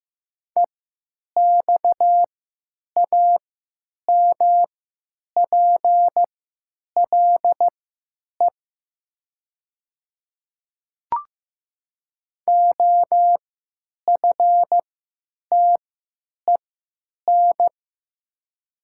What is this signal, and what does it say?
Morse code, heard as a keyed tone, about 15 wpm, spaced as for 9 wpm, 700 Hz. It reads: EXAMPLE OFTEN
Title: 3x Character Spacing - Top 300 Words - 15wpm